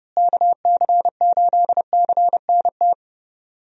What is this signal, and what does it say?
Morse code, heard as a keyed tone, about 30 wpm, 700 Hz. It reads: KC8CNT
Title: USA Call Signs - 6 Characters - Repeated 2x - 30wpm